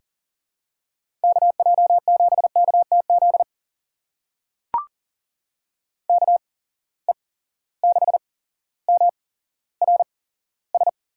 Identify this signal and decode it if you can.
Morse code, heard as a keyed tone, about 38 wpm, spaced as for 17 wpm, 700 Hz. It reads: KJ7KTZ KE6MRS
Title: USA Call Signs - 6 Characters - 8x Character Spacing - 40wpm